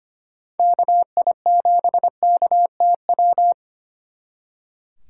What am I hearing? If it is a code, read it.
Morse code, heard as a keyed tone, about 31 wpm, 700 Hz. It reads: KI7KTW
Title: Speed Racing - USA 2x3 Call Signs 25wpm